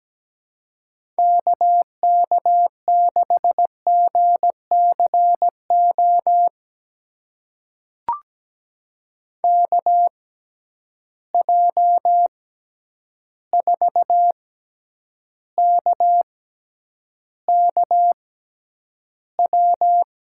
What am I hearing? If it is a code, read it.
Morse code, heard as a keyed tone, about 17 wpm, spaced as for 9 wpm, 700 Hz. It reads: KK6GCO KJ4KKW
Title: USA Call Signs - 6 Characters - 6x Character Spacing - 17wpm